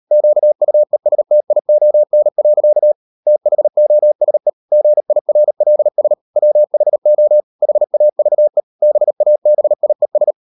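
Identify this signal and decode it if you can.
Morse code, heard as a keyed tone, about 38 wpm, 600 Hz. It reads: QUESTION. THOSE GIRLS WHO HAVE BABIES
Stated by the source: Morse code practice at 38 WPM, sending Ken Follett's novel Never in English, from CW med Gnister.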